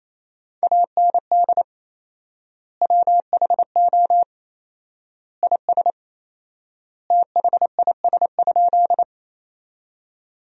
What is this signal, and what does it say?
Morse code, heard as a keyed tone, about 28 wpm, 700 Hz. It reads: AND WHO IS THIS?
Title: 4x Word-Spacing - Sentences from Top 100 Words - 28wpm